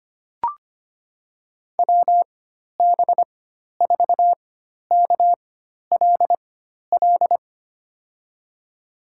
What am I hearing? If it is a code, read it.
Morse code, heard as a keyed tone, about 25 wpm, spaced as for 12 wpm, 700 Hz. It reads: WB4KLL
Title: USA Call Signs - 6 Characters - 4x Character Spacing - 25wpm